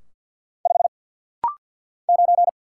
Morse code, heard as a keyed tone, about 50 wpm, 700 Hz. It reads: 5 9
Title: Single Letter-Number - Rapid-Fire 50wpm